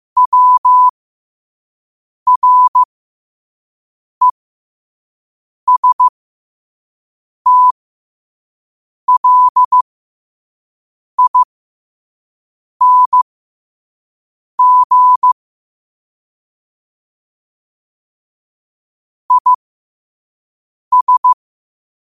Morse code, heard as a keyed tone, about 15 wpm, spaced as for 5 wpm, 1000 Hz. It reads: WRESTLING IS
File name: qotdfiles / 5wpm